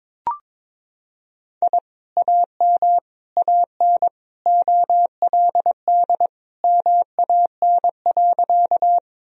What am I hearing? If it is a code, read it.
Morse code, heard as a keyed tone, about 21 wpm, 700 Hz. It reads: I AM AN OLD MAN.
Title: Sentences from Top 1000 Words - Review Part 2 - 22wpm